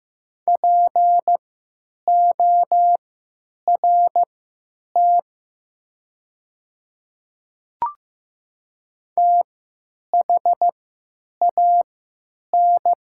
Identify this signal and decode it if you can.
Morse code, heard as a keyed tone, about 15 wpm, spaced as for 9 wpm, 700 Hz. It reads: PORT THAN
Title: 3x Character Spacing - Top 300 Words - 15wpm